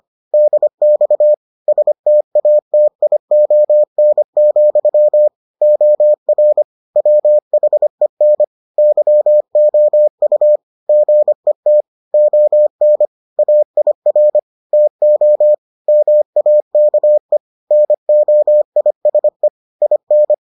Morse code, heard as a keyed tone, about 25 wpm, 600 Hz. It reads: DX STATION, OR WHEN YOU GET ON AIR TO MAKE NOISE IN